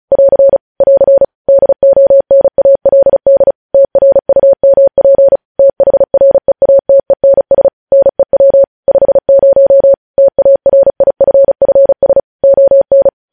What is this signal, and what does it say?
Morse code, heard as a keyed tone, about 36 wpm, 550 Hz. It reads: <AR> <AR> DONALD TRUMP THREATENS NEW 50 TARIFFS ON
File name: podcast / News headlines-1x-35x35